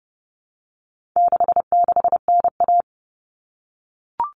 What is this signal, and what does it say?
Morse code, heard as a keyed tone, about 30 wpm, 700 Hz. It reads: 66NA
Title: All Cut Numbers - 4-Digit 30wpm